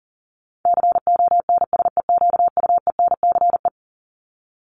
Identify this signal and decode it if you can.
Morse code, heard as a keyed tone, about 37 wpm, 700 Hz. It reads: CONSEQUENCE
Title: LY Suffix - Top 100 Words with Base Words 40wpm